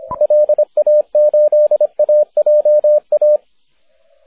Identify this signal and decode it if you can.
Morse code, heard as a keyed tone, about 28 wpm, 560 Hz, with radio noise. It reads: LA8AJA
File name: wp / e4x 6m cw may 30th 2010